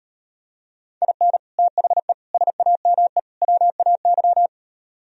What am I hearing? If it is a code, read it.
Morse code, heard as a keyed tone, about 39 wpm, 700 Hz. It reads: IN THE SAME WAY